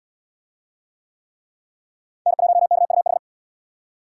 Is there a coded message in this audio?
Yes: Morse code from 2.3 to 3.2 s, heard as a keyed tone, about 77 wpm, 700 Hz. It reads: N1KUH